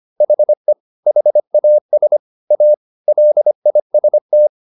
Morse code, heard as a keyed tone, about 25 wpm, 600 Hz. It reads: HE HAS A LIST